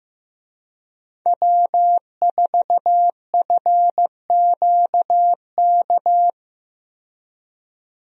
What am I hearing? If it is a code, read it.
Morse code, heard as a keyed tone, about 16 wpm, 700 Hz. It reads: W4FQK
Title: USA Call Signs - 5 Characters - 15wpm